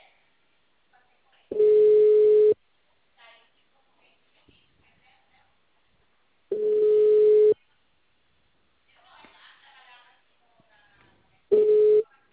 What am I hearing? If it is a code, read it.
Ringback tone (Europe, 425 Hz), with speech.
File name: out- -2015- - - .56